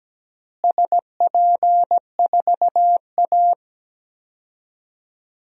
Morse code, heard as a keyed tone, about 17 wpm, 700 Hz. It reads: SP4A
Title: DXCC-100 Call Signs - 4 Characters - Repeated 3x - 17wpm